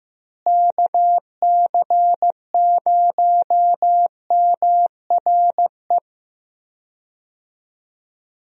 Morse code, heard as a keyed tone, about 15 wpm, 700 Hz. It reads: KC0MRE
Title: USA Call Signs - 6 Characters - Repeated 2x - 15wpm